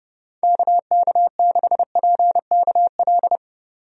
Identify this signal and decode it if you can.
Morse code, heard as a keyed tone, about 29 wpm, 700 Hz. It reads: KK6PKL
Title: USA Call Signs - 6 Characters - 30wpm